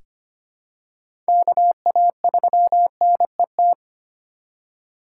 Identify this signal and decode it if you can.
Morse code, heard as a keyed tone, about 25 wpm, 700 Hz. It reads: KA3NET
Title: USA Call Signs - 6 Characters - 3x Character Spacing - 25wpm